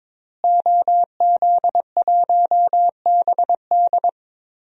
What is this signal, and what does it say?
Morse code, heard as a keyed tone, about 22 wpm, 700 Hz. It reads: OZ1BD
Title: DXCC-100 Call Signs - 5 Characters - Repeated 2x - 22wpm